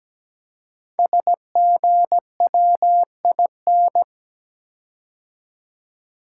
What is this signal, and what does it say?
Morse code, heard as a keyed tone, about 17 wpm, 700 Hz. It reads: SGWIN
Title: Code Groups - 5 Letters - 17wpm